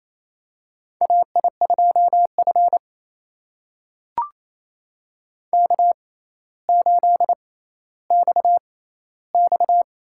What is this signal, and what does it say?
Morse code, heard as a keyed tone, about 28 wpm, spaced as for 14 wpm, 700 Hz. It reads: AI2F K8XX
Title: USA Call Signs - 4 Characters - 6x Character Spacing - 28wpm